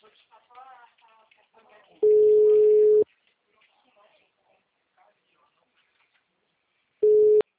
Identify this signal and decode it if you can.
Ringback tone (Europe, 425 Hz), with speech.
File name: out- -2002- - - .8